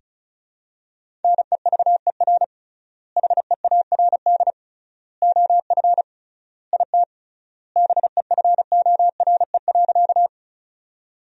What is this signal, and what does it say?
Morse code, heard as a keyed tone, about 36 wpm, 700 Hz. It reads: NEVER HEARD OF IT BEFORE.
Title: 3x Word-Spacing - Sentences from Top 400 Words - 35wpm